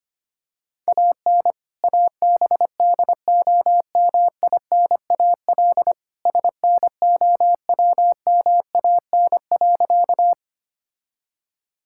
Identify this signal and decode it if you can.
Morse code, heard as a keyed tone, about 25 wpm, 700 Hz. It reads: AN ABDOMINAL SNOWMAN.
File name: practice / 25wpm / All-Age Appropriate Jokes - 25wpm